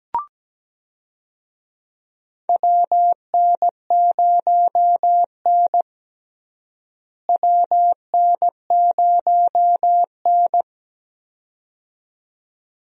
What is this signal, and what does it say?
Morse code, heard as a keyed tone, about 17 wpm, 700 Hz. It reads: WN0N WN0N
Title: USA Call Signs - 4 Characters - Repeated 2x - 17wpm